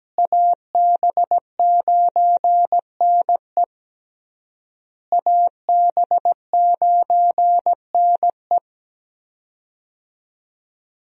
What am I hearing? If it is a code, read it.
Morse code, heard as a keyed tone, about 17 wpm, 700 Hz. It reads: AB9NE AB9NE
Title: USA Call Signs - 5 Characters - Repeated 3x - 17wpm